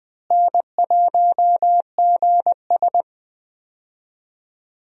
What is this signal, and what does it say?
Morse code, heard as a keyed tone, about 20 wpm, 700 Hz. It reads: N1GS
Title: USA Call Signs - 4 Characters - Repeated 2x - 20wpm